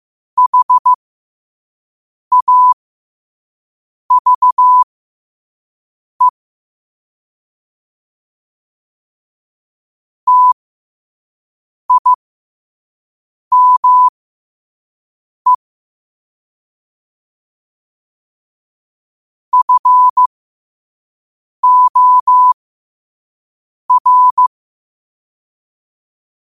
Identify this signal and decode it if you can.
Morse code, heard as a keyed tone, about 15 wpm, spaced as for 5 wpm, 1000 Hz. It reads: HAVE TIME FOR
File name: qotdfiles / 5wpm